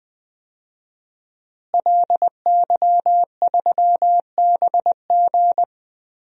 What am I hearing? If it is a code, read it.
Morse code, heard as a keyed tone, about 21 wpm, 700 Hz. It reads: LY3BG
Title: DXCC-100 Call Signs - 5 Characters - Repeated 2x - 20wpm